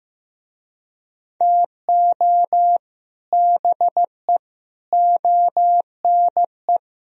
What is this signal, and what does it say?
Morse code, heard as a keyed tone, about 15 wpm, 700 Hz. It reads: TO BE ONE
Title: Speed-Racing - Sets of 5 Words 15wpm